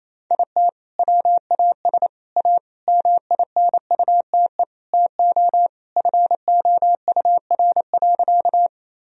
Morse code, heard as a keyed tone, about 28 wpm, 700 Hz. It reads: IT WAS A MINUTE TO FOUR.
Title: Sentences from Top 1000 Words - Review Part 1 - 28wpm